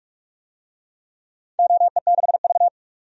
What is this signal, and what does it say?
Morse code, heard as a keyed tone, about 42 wpm, 700 Hz. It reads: OE6V
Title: DXCC-100 Call Signs - 4 Characters - 45wpm